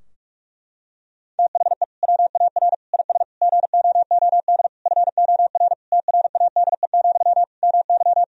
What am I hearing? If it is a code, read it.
Morse code, heard as a keyed tone, about 42 wpm, 700 Hz. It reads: THE WAR IS GOOD FOR TRADE, MY
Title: Sentences from Top 1000 Words - Review Part 4 - 45wpm